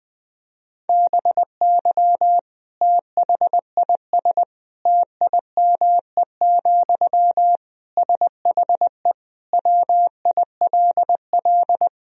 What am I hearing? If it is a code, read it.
Morse code, heard as a keyed tone, about 20 wpm, 700 Hz. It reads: BY THIS TIME, SHE WILL
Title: Sentences from Top 1000 Words - Review Part 4 - 20wpm